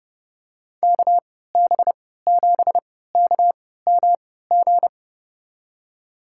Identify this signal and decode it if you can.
Morse code, heard as a keyed tone, about 32 wpm, spaced as for 17 wpm, 700 Hz. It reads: KB7KMG
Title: USA Call Signs - 6 Characters - 3x Character Spacing - 30wpm